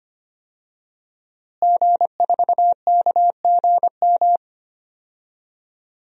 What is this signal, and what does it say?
Morse code, heard as a keyed tone, about 25 wpm, 700 Hz. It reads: G4KGM
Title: DXCC-100 Call Signs - 5 Characters - Repeated 2x - 25wpm